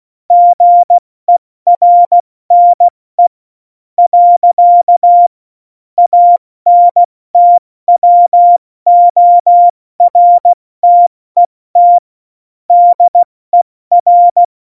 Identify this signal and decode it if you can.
Morse code, heard as a keyed tone, about 16 wpm, 700 Hz. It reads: GERNE . ANTWORTET DER
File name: Morsekurs / Witze / Witz 06 March 2018, 70 F80